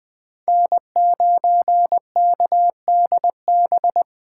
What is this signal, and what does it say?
Morse code, heard as a keyed tone, about 20 wpm, 700 Hz. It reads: N9KDB
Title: K1USN SST QSO Practice - 20wpm